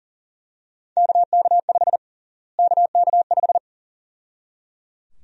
Morse code, heard as a keyed tone, about 37 wpm, 700 Hz. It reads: KK5 KK5
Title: USA Call Sign Prefixes - Repeated 2x - 40wpm